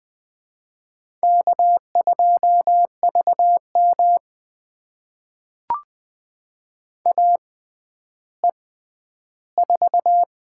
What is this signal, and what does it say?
Morse code, heard as a keyed tone, about 20 wpm, spaced as for 11 wpm, 700 Hz. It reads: K2VM AE4
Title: USA Call Signs - 6x Character Spacing - 20wpm